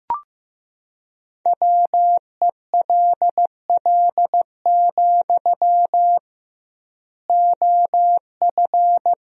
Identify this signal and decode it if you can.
Morse code, heard as a keyed tone, about 15 wpm, 700 Hz. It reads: WELL, OF
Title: 2x Word-Spacing - Sentences from Top 500 Words - 15wpm